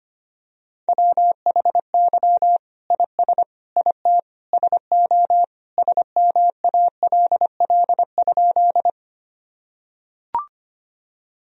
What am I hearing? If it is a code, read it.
Morse code, heard as a keyed tone, about 25 wpm, 700 Hz. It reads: WHY IS IT SO SMALL?
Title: Sentences from Top 1000 Words - Review Part 1 - 25wpm